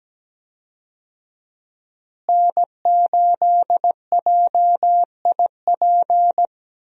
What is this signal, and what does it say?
Morse code, heard as a keyed tone, about 17 wpm, 700 Hz. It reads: N8JIP
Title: USA Call Signs - 5 Characters - Repeated 2x - 17wpm